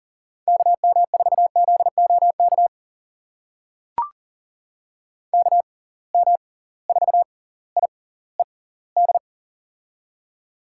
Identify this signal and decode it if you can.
Morse code, heard as a keyed tone, about 40 wpm, spaced as for 21 wpm, 700 Hz. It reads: KM4ZOK KM4IED
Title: USA Call Signs - 6 Characters - 6x Character Spacing - 40wpm